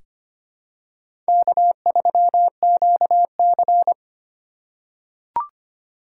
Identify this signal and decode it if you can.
Morse code, heard as a keyed tone, about 25 wpm, 700 Hz. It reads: K3QC
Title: USA Call Signs - 8x Character Spacing - 25wpm